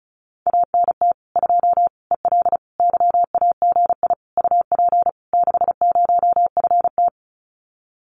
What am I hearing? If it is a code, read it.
Morse code, heard as a keyed tone, about 36 wpm, 700 Hz. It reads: ANT 2 EL YAGI UP 60FT
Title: Common QSO Phrases Encore 35wpm